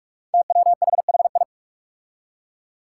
Morse code, heard as a keyed tone, about 46 wpm, 700 Hz. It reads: TWHHI
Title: Five Challenging Letter-Number Combinations 45wpm